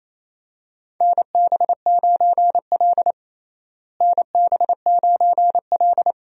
Morse code, heard as a keyed tone, about 28 wpm, 700 Hz. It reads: NB9L NB9L
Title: USA Call Signs - 4 Characters - Repeated 2x - 28wpm